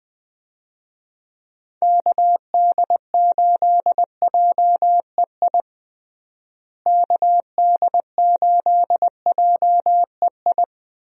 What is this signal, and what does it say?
Morse code, heard as a keyed tone, about 20 wpm, 700 Hz. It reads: KD8JEI KD8JEI
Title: USA Call Signs - 6 Characters - Repeated 3x - 20wpm